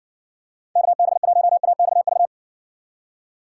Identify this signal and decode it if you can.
Morse code, heard as a keyed tone, about 56 wpm, 700 Hz. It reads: KB1AXV